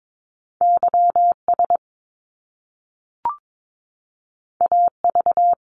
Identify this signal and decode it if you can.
Morse code, heard as a keyed tone, about 22 wpm, 700 Hz. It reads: YS AV
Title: Top Two Letter Combinations in Words 22wpm